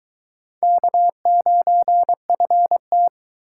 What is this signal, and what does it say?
Morse code, heard as a keyed tone, about 22 wpm, 700 Hz. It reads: K9FT